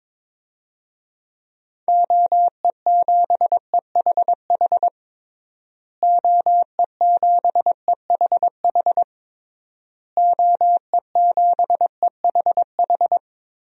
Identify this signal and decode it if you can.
Morse code, heard as a keyed tone, about 22 wpm, 700 Hz. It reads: OE7EHH OE7EHH OE7EHH
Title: DXCC-100 Call Signs - 6 Characters - Repeated 3x - 22wpm